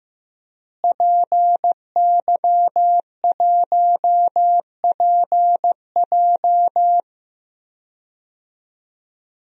Morse code, heard as a keyed tone, about 16 wpm, 700 Hz. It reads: PY1PJ
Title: DXCC-100 Call Signs - 5 Characters - Repeated 3x - 15wpm